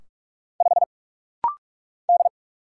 Morse code, heard as a keyed tone, about 44 wpm, 700 Hz. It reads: 5 D
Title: Single Letter-Number - Rapid-Fire 45wpm